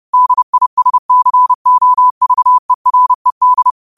Morse code, heard as a keyed tone, about 30 wpm, 1000 Hz. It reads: DISCOVERED
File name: qotdfiles / 30wpm